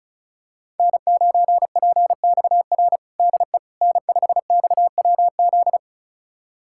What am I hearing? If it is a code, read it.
Morse code, heard as a keyed tone, about 34 wpm, 700 Hz. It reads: N9PXR DE N5XWZ
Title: USA Callsign DE Callsign - 5 Characters - 35wpm